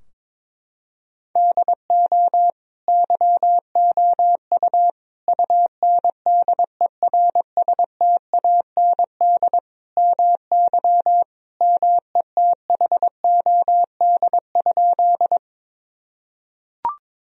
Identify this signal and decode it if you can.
Morse code, heard as a keyed tone, about 22 wpm, 700 Hz. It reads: DO YOU UNDERSTAND MY METHOD?
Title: Sentences from Top 1000 Words - Review Part 3 - 22wpm